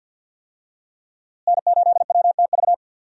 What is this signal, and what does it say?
Morse code, heard as a keyed tone, about 52 wpm, 700 Hz. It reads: N8WTV
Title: USA Call Signs - 5 Characters - Repeated 2x - 50wpm